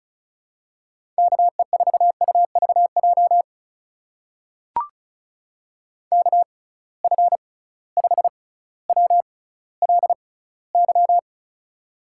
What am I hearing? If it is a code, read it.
Morse code, heard as a keyed tone, about 36 wpm, spaced as for 18 wpm, 700 Hz. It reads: KE4UVJ KF5WLY
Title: USA Call Signs - 6 Characters - 6x Character Spacing - 35wpm